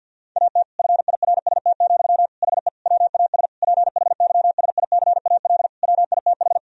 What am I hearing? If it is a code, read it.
Morse code, heard as a keyed tone, about 52 wpm, 700 Hz. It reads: AT FIRST, HE WAS PHYSICAL WITH
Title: Sentences from Top 1000 Words - Review Part 4 - 50wpm